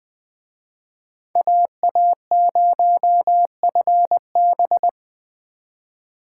Morse code, heard as a keyed tone, about 21 wpm, 700 Hz. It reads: AA0FB